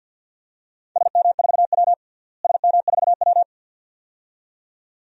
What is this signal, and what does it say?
Morse code, heard as a keyed tone, about 52 wpm, 700 Hz. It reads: SM4W SM4W
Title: DXCC-100 Call Signs - 4 Characters - Repeated 2x - 50wpm